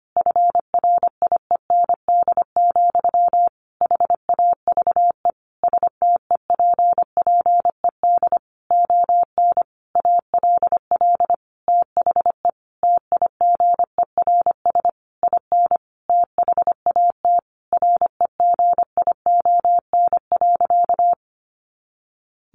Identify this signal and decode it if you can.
Morse code, heard as a keyed tone, about 25 wpm, 700 Hz. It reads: FRIEND, HAVE STEPPED ON ALL THE TIGERS IN THAT REGION.
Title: A Sound Of Thunder 25wpm